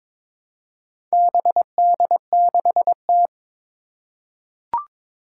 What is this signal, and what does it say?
Morse code, heard as a keyed tone, about 21 wpm, 700 Hz. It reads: BD6T
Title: Four Challenging Letter-Number Combinations 22wpm